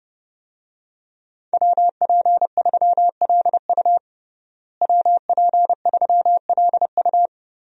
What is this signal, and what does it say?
Morse code, heard as a keyed tone, about 30 wpm, 700 Hz. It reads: WP3LU WP3LU
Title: USA Call Signs - 5 Characters - Repeated 2x - 30wpm